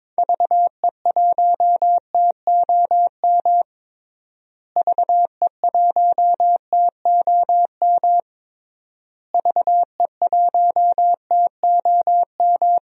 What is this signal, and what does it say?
Morse code, heard as a keyed tone, about 22 wpm, 700 Hz. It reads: VE1TOM VE1TOM VE1TOM
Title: DXCC-100 Call Signs - 6 Characters - Repeated 3x - 22wpm